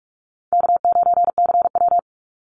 Morse code, heard as a keyed tone, about 46 wpm, 700 Hz. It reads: K9CW
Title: Easy Call Signs - 45wpm